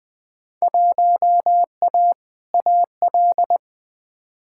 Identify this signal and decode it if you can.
Morse code, heard as a keyed tone, about 21 wpm, 700 Hz. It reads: 1A AL
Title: Field Day Exchanges - 20wpm